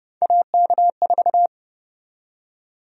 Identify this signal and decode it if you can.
Morse code, heard as a keyed tone, about 30 wpm, 700 Hz. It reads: AK4
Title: USA Call Sign Prefixes - 30wpm